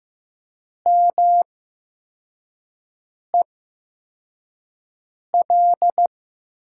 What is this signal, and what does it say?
Morse code, heard as a keyed tone, about 15 wpm, spaced as for 4 wpm, 700 Hz. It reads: MEL
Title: USA Call Signs - 6 Characters - 8x Character Spacing - 15wpm